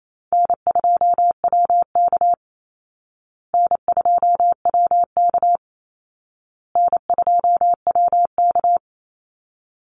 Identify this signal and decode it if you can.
Morse code, heard as a keyed tone, about 28 wpm, 700 Hz. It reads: N2WK N2WK N2WK
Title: Call Signs Repeated 2x - 28wpm